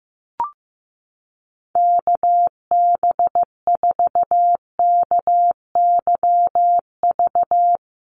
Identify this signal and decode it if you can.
Morse code, heard as a keyed tone, about 15 wpm, 700 Hz. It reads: KB4KYV
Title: USA Call Signs 15wpm